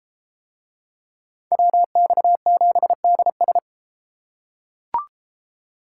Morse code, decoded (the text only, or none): WX7DS